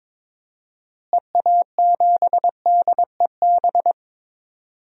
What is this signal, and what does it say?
Morse code, heard as a keyed tone, about 21 wpm, 700 Hz. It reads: EA7DEB
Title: DXCC-100 Call Signs - 6 Characters - Repeated 2x - 22wpm